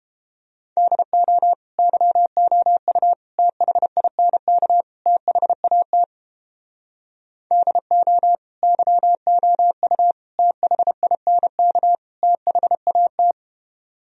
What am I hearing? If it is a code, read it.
Morse code, heard as a keyed tone, about 31 wpm, 700 Hz. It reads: DO YOU THINK THAT DO YOU THINK THAT